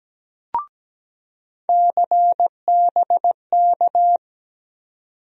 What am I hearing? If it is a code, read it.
Morse code, heard as a keyed tone, about 17 wpm, 700 Hz. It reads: CBK
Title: USA Call Sign Postfixes - 3 Letters 17wpm